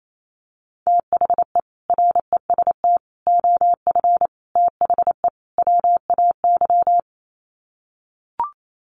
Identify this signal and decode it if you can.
Morse code, heard as a keyed tone, about 28 wpm, 700 Hz. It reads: THE REST OF THE WAY
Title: Sets of 5 Words Encore 28wpm